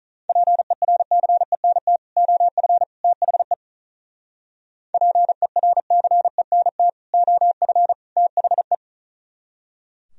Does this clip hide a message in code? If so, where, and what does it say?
Yes: Morse code from 0.3 to 8.7 s, heard as a keyed tone, about 45 wpm, 700 Hz. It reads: PERCENT OF THE PERCENT OF THE